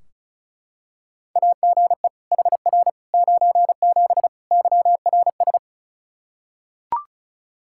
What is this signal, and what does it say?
Morse code, heard as a keyed tone, about 36 wpm, 700 Hz. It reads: AGE HR 97 YRS
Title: Practice QSO - Exchange 3 - More Info - 35wpm